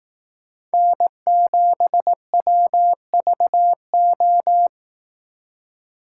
Morse code, heard as a keyed tone, about 18 wpm, 700 Hz. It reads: N7WVO
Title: Speed Racing - USA 1x3 Call Signs 15wpm